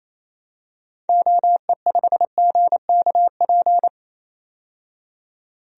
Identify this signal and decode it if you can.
Morse code, heard as a keyed tone, about 28 wpm, 700 Hz. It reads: OE5GKP
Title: DXCC-100 Call Signs - 6 Characters - 28wpm